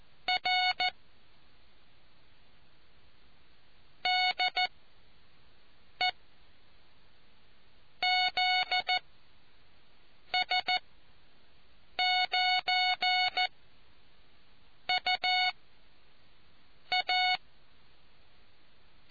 Morse code, heard as a keyed tone, about 14 wpm, spaced as for 5 wpm, 730 Hz. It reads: R DE ZS9UA